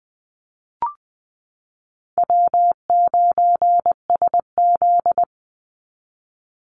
Morse code, heard as a keyed tone, about 20 wpm, 700 Hz. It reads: W9SZ
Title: Easy Call Signs - 20wpm